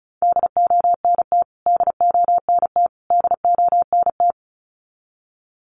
Morse code, heard as a keyed tone, about 34 wpm, 700 Hz. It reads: DONT DONT DONT
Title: Top 300 Words Repeated 35wpm